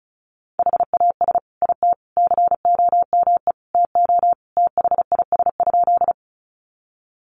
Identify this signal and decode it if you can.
Morse code, heard as a keyed tone, about 34 wpm, 700 Hz. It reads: HAS IT COME TO THIS?
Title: Sentences from Top 100 Words 35wpm